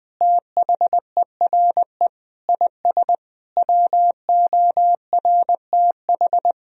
Morse code, heard as a keyed tone, about 21 wpm, 700 Hz. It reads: THERE IS WORTH
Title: Sentences from Top 1000 Words - Review Part 1 - 20wpm